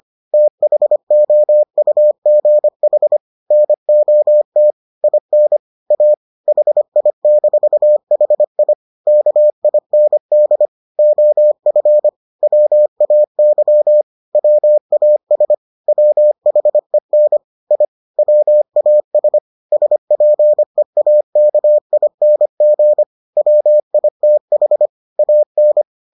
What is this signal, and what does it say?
Morse code, heard as a keyed tone, about 25 wpm, 600 Hz. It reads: THOUGH NOT IN A HI-HI KIND OF WAY WAS WHEN I WAS SPEAKING WITH AN